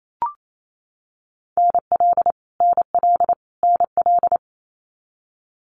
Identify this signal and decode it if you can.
Morse code, heard as a keyed tone, about 28 wpm, 700 Hz. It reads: NL NL NL
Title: Canadian Provinces and Territories - Repeated 3x - 28wpm